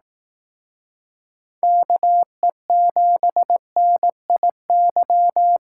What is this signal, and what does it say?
Morse code, heard as a keyed tone, about 18 wpm, 700 Hz. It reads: KE7NIY
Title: Speed Racing - USA 2x3 Call Signs 15wpm